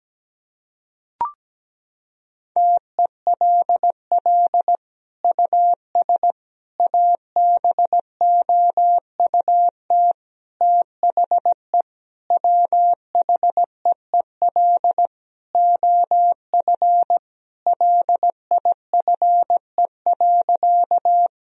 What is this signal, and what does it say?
Morse code, heard as a keyed tone, about 17 wpm, 700 Hz. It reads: TELL US ABOUT THE WHEEL OF LIFE.
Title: Sentences from Top 1000 Words - Review Part 3 - 17wpm